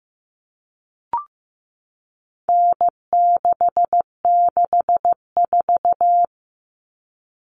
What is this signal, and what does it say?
Morse code, heard as a keyed tone, about 15 wpm, 700 Hz. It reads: N664
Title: All Cut Numbers - 4-Digit 15wpm